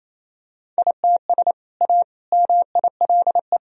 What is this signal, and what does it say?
Morse code, heard as a keyed tone, about 28 wpm, 700 Hz. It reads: ITS A MILE
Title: Sentences from Top 1000 Words - Review Part 1 - 28wpm